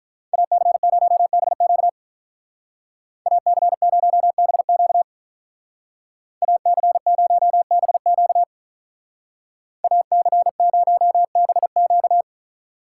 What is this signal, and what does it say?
Morse code, heard as a keyed tone, about 42 wpm, 700 Hz. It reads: AC0BQ AC0BQ AC0BQ AC0BQ
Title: Speed Racing - USA Call Signs 35wpm